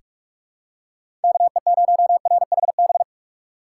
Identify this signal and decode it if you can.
Morse code, heard as a keyed tone, about 46 wpm, 700 Hz. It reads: KE0RHB
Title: USA Call Signs - 6 Characters - 6x Character Spacing - 45wpm